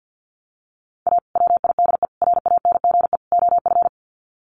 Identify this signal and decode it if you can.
Morse code, heard as a keyed tone, about 50 wpm, 700 Hz. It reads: A WIDE RANGE OF
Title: Sets of 4 Words 50wpm